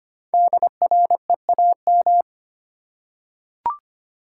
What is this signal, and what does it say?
Morse code, heard as a keyed tone, about 25 wpm, 700 Hz. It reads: DREAM
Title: Five Letter Words - 25wpm